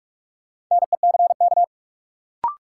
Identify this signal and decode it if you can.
Morse code, heard as a keyed tone, about 43 wpm, 700 Hz. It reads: NECK
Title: Sets of 5 Words Encore - New Word Drill - 45wpm